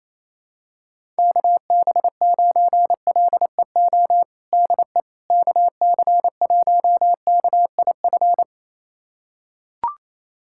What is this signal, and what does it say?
Morse code, heard as a keyed tone, about 28 wpm, 700 Hz. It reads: KB9LEO DE KC1KIF
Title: USA Callsign DE Callsign - 6 Characters - 28wpm